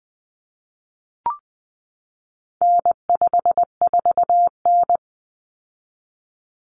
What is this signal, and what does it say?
Morse code, heard as a keyed tone, about 20 wpm, 700 Hz. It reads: N54N
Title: Common Cut Numbers - 4-Digit 20wpm